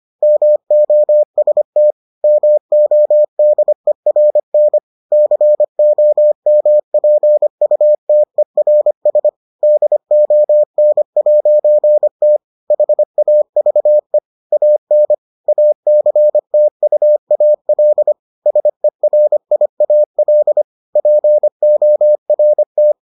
Morse code, heard as a keyed tone, about 25 wpm, 600 Hz. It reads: MOST MODERN COMPUTERS DON'T HAVE AN ACTUAL SERIAL PORT